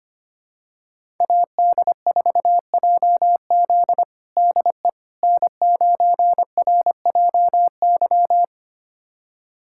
Morse code, heard as a keyed tone, about 25 wpm, 700 Hz. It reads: AD4JZ DE N9RJY
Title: USA Callsign DE Callsign - 5 Characters - 25wpm